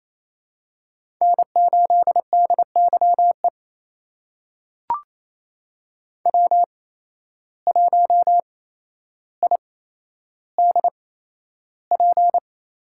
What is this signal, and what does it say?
Morse code, heard as a keyed tone, about 28 wpm, spaced as for 12 wpm, 700 Hz. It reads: N8DYE W1IDP
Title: USA Call Signs - 5 Characters - 8x Character Spacing - 28wpm